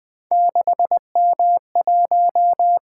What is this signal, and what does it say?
Morse code, heard as a keyed tone, about 20 wpm, 700 Hz. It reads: 6M1
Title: Three Letter-Number Combinations - 20wpm